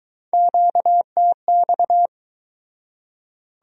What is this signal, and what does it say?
Morse code, heard as a keyed tone, about 23 wpm, 700 Hz. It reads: QTX